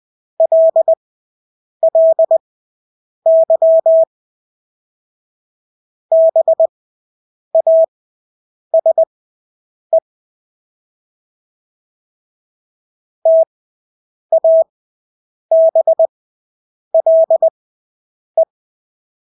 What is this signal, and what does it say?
Morse code, heard as a keyed tone, about 20 wpm, spaced as for 8 wpm, 650 Hz. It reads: LLY BASE TABLE
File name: Engelske ord 20/8 wpm